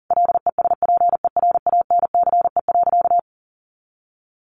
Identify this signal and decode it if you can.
Morse code, heard as a keyed tone, about 40 wpm, 700 Hz. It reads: LESPERANCE.
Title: A Sound Of Thunder 40wpm